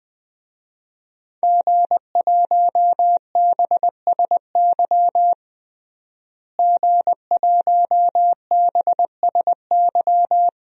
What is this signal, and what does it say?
Morse code, heard as a keyed tone, about 21 wpm, 700 Hz. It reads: G1BSY G1BSY
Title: DXCC-100 Call Signs - 5 Characters - Repeated 3x - 20wpm